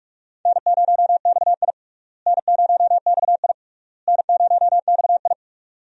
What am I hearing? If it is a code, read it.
Morse code, heard as a keyed tone, about 42 wpm, 700 Hz. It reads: N0XI N0XI N0XI
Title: USA Call Signs - 4 Characters - Repeated 3x - 45wpm